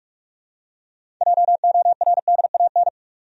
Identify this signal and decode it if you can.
Morse code, heard as a keyed tone, about 43 wpm, 700 Hz. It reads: JORDAN
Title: USA Common Names - 45wpm